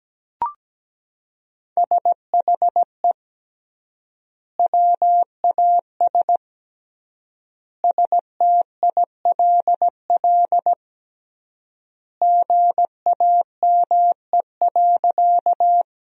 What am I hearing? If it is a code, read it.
Morse code, heard as a keyed tone, about 17 wpm, 700 Hz. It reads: SHE WAS STILL GAME.
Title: 3x Word-Spacing - Sentences from Top 500 Words - 17wpm